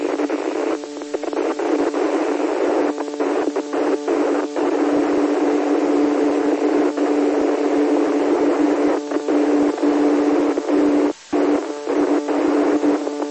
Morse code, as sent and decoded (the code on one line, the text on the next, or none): none